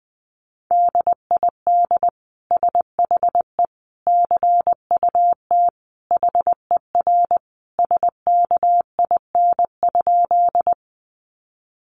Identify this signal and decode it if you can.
Morse code, heard as a keyed tone, about 19 wpm, 700 Hz. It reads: DID SHE CUT HER SKIN?
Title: Sentences from Top 1300 Words 20wpm